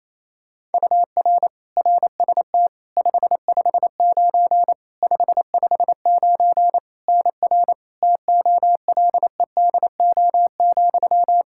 Morse code, heard as a keyed tone, about 27 wpm, 700 Hz. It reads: UR RST 559 559 NR TOLEDO,
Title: Practice QSO - Exchange 1 - Beginning - 28wpm